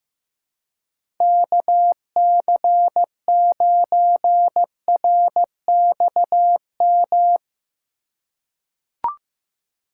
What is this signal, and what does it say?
Morse code, heard as a keyed tone, about 15 wpm, 700 Hz. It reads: KC9RXM
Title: USA Call Signs - 6 Characters - 2x Character Spacing - 15wpm